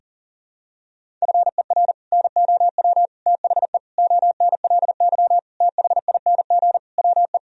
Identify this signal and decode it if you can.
Morse code, heard as a keyed tone, about 40 wpm, 700 Hz. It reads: FER NOW THE ONLY THING WE
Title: Sentences from Top 200 Words and Abbreviations 40wpm